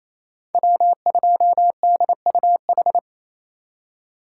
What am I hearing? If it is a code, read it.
Morse code, heard as a keyed tone, about 28 wpm, 700 Hz. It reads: W2DUH
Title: USA Call Signs - 5 Characters - 2x Character Spacing - 28wpm